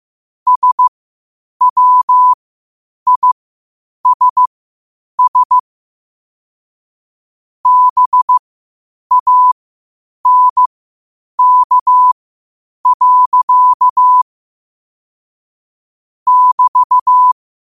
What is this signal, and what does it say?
Morse code, heard as a keyed tone, about 15 wpm, spaced as for 8 wpm, 1000 Hz. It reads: SWISS BANK. <BT>
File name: qotdfiles / 8wpm